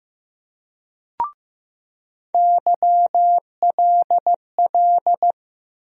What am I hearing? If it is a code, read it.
Morse code, heard as a keyed tone, about 15 wpm, 700 Hz. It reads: YLL
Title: USA Call Sign Postfixes - 3 Letters 15wpm